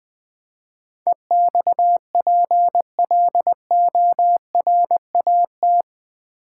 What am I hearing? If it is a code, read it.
Morse code, heard as a keyed tone, about 20 wpm, 700 Hz. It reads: EXPLORAT
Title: Build A Word - 4 Letters At A Time - 11-Letter Words - 20wpm